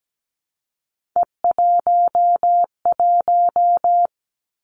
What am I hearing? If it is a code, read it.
Morse code, heard as a keyed tone, about 17 wpm, 700 Hz. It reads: E11
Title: Three Challenging Letter-Number Combinations 17wpm